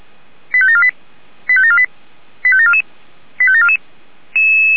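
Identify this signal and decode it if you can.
Five-tone selective call (ZVEI), heard as quick runs of five tones, tones of about 70 ms. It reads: 85748 85748 85744 85744
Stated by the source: ZVEI five-tone selective call received on SDR.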